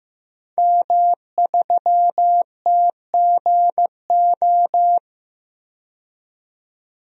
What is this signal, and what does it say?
Morse code, heard as a keyed tone, about 15 wpm, 700 Hz. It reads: M3TGO
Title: DXCC-100 Call Signs - 5 Characters - 15wpm